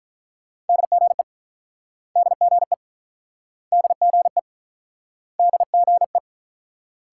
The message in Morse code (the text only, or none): DGE DGE DGE DGE